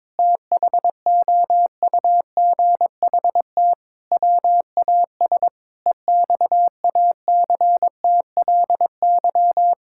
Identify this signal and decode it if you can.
Morse code, heard as a keyed tone, about 22 wpm, 700 Hz. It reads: THOUGHT WAS EXACTLY
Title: Sentences from Top 1000 Words - Review Part 6 - 22wpm